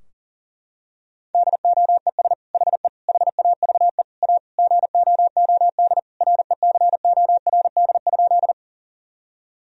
Morse code, heard as a keyed tone, about 40 wpm, 700 Hz. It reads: DOES HE HAVE A GOOD RECORD?
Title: Sentences from Top 1000 Words - Review Part 3 - 40wpm